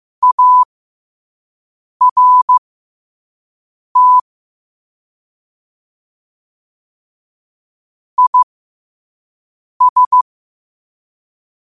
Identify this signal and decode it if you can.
Morse code, heard as a keyed tone, about 15 wpm, spaced as for 5 wpm, 1000 Hz. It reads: ART IS
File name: qotdfiles / 5wpm